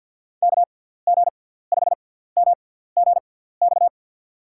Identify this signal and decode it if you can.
Morse code, heard as a keyed tone, about 52 wpm, spaced as for 17 wpm, 700 Hz. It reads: KG5MGX
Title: USA Call Signs - 6 Characters - 6x Character Spacing - 50wpm